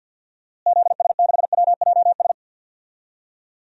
Morse code, heard as a keyed tone, about 52 wpm, 700 Hz. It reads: ZS6WJS
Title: DXCC-100 Call Signs - 6 Characters - Repeated 3x - 50wpm